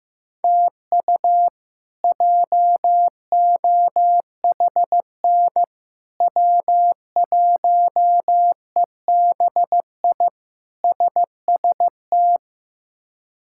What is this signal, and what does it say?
Morse code, heard as a keyed tone, about 16 wpm, 700 Hz. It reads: TU JOHN W1EBI SST
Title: K1USN SST QSO Practice - 15wpm